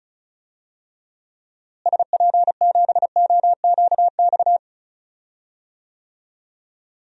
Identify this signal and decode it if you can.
Morse code, heard as a keyed tone, about 36 wpm, 700 Hz. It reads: SP7OQX
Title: DXCC-100 Call Signs - 6 Characters - Repeated 2x - 35wpm